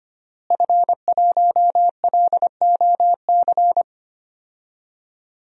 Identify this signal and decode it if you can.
Morse code, heard as a keyed tone, about 25 wpm, 700 Hz. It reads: F1LOC
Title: DXCC-100 Call Signs - 5 Characters - Repeated 3x - 25wpm